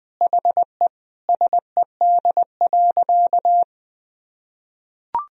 Morse code, heard as a keyed tone, about 20 wpm, 700 Hz. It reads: HE SED.
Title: Sentences from Top 100 Words and Abbreviations 20wpm